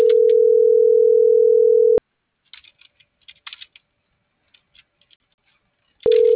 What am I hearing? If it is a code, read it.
Ringback tone (North America), 440 + 480 Hz.